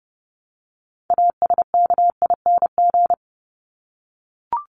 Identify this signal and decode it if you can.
Morse code, heard as a keyed tone, about 30 wpm, 700 Hz. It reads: ASKING
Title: ING Suffix - Top 100 Words with Base Words 30wpm